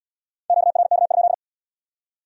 Morse code, heard as a keyed tone, about 68 wpm, 700 Hz. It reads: QRK?